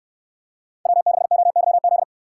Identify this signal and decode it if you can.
Morse code, heard as a keyed tone, about 63 wpm, 700 Hz. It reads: W6OYZ